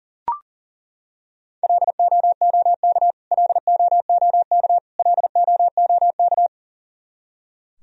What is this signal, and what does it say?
Morse code, heard as a keyed tone, about 40 wpm, 700 Hz. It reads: LOOK LOOK LOOK
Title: Top 100 Words Repeated 40wpm